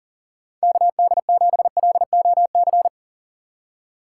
Morse code, heard as a keyed tone, about 40 wpm, 700 Hz. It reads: KD7LOC
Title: USA Call Signs - 6 Characters - 4x Character Spacing - 40wpm